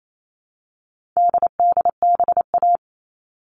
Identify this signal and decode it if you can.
Morse code, heard as a keyed tone, about 28 wpm, 700 Hz. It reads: DDBA
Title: All Cut Numbers - 4-Digit 28wpm